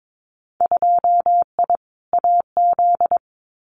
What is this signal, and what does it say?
Morse code, heard as a keyed tone, about 22 wpm, 700 Hz. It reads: 2I AZ
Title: Winter Field Day Exchanges Encore 22wpm